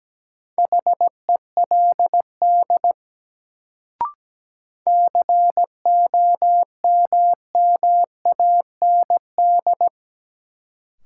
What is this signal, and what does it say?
Morse code, heard as a keyed tone, about 17 wpm, 700 Hz. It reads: HELD COMMAND
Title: Sentences from Top 1600 Words - New Word Drill Alt - 17wpm